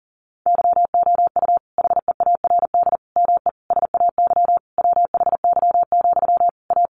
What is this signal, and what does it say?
Morse code, heard as a keyed tone, about 37 wpm, 700 Hz. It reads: YOU HEARD ME SAY WHY, A
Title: Sentences from Top 500 Words 40wpm